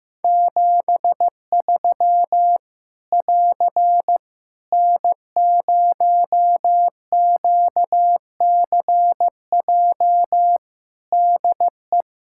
Morse code, heard as a keyed tone, about 16 wpm, 700 Hz. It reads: 73 <AR> N0QCJ DE
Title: K1USN SST QSO Practice 15wpm